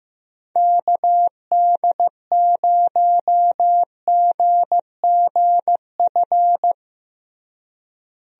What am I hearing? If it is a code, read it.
Morse code, heard as a keyed tone, about 15 wpm, 700 Hz. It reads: KD0GGF
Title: USA Call Signs - 6 Characters - Repeated 2x - 15wpm